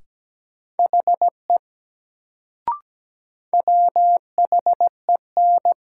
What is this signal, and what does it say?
Morse code, heard as a keyed tone, about 17 wpm, 700 Hz. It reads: HE WHEN
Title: Sentences from Top 100 Words - New Word Drill - 17wpm